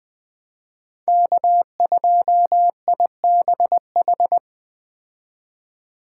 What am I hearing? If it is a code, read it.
Morse code, heard as a keyed tone, about 20 wpm, 700 Hz. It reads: K2IBH
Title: Speed Racing - USA 1x3 Call Signs 17wpm